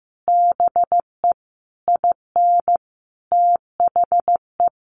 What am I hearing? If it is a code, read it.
Morse code, heard as a keyed tone, about 15 wpm, 700 Hz. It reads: BE IN THE
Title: Sets of 5 Words Encore 15wpm